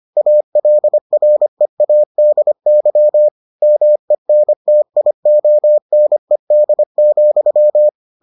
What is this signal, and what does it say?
Morse code, heard as a keyed tone, about 25 wpm, 600 Hz. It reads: ALREADY MENTIONED,